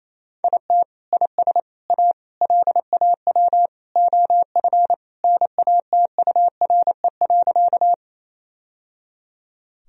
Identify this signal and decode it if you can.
Morse code, heard as a keyed tone, about 28 wpm, 700 Hz. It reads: IT IS A LAW OF NATURE.
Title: Sentences from Top 1000 Words - Review Part 3 - 28wpm